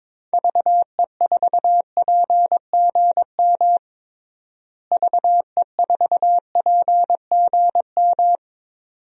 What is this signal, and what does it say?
Morse code, heard as a keyed tone, about 22 wpm, 700 Hz. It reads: VE4PGM VE4PGM
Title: DXCC-100 Call Signs - 6 Characters - Repeated 2x - 22wpm